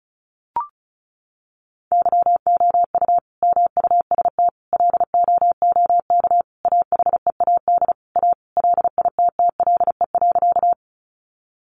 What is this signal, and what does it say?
Morse code, heard as a keyed tone, about 34 wpm, 700 Hz. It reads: YOU MUST LOOK AHEAD A LITTLE.
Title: Sentences from Top 1100 Words 35wpm